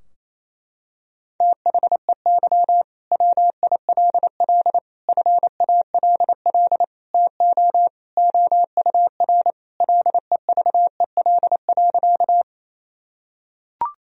Morse code, heard as a keyed tone, about 28 wpm, 700 Hz. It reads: THEY WILL FALL TO OUR LEVEL.
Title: Sentences from Top 1000 Words - Review Part 4 - 28wpm